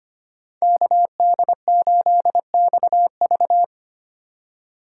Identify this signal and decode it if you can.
Morse code, heard as a keyed tone, about 25 wpm, 700 Hz. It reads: KD8XV
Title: USA Call Signs - 5 Characters - 4x Character Spacing - 25wpm